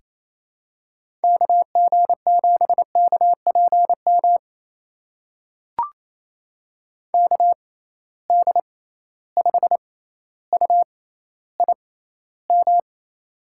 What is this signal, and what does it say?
Morse code, heard as a keyed tone, about 28 wpm, spaced as for 14 wpm, 700 Hz. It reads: KG7KPM KD5UIM
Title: USA Call Signs - 6 Characters - 6x Character Spacing - 28wpm